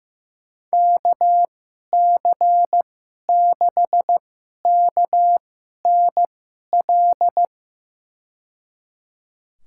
Morse code, heard as a keyed tone, about 15 wpm, spaced as for 11 wpm, 700 Hz. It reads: KC6KNL
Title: USA Call Signs - 6 Characters - 2x Character Spacing - 15wpm